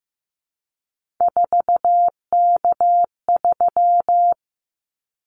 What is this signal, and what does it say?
Morse code, heard as a keyed tone, about 15 wpm, 700 Hz. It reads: 4K3
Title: Callsign Prefixes for Top 100 DXCC - 15wpm